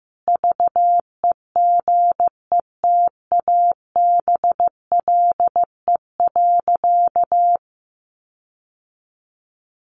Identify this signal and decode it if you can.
Morse code, heard as a keyed tone, about 14 wpm, 700 Hz. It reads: VEGETABLE.
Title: Sentences from Top 1300 Words 15wpm